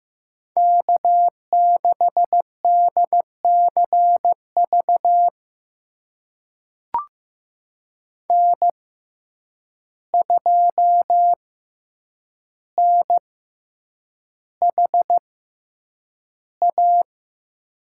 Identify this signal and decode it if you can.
Morse code, heard as a keyed tone, about 16 wpm, spaced as for 8 wpm, 700 Hz. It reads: K6DCV N2NHA
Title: USA Call Signs - 5 Characters - 6x Character Spacing - 15wpm